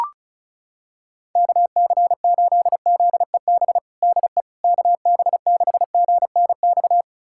Morse code, heard as a keyed tone, about 34 wpm, 700 Hz. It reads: KC8ZEB DE KB6GNX
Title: USA Callsign DE Callsign 35wpm